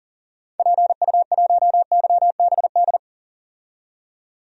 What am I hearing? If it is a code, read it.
Morse code, heard as a keyed tone, about 40 wpm, 700 Hz. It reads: PU1YBD
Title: DXCC-100 Call Signs - 6 Characters - 40wpm